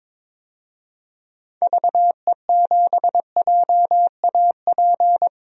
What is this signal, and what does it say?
Morse code, heard as a keyed tone, about 22 wpm, 700 Hz. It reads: VE7JAP
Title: DXCC-100 Call Signs - 6 Characters - 22wpm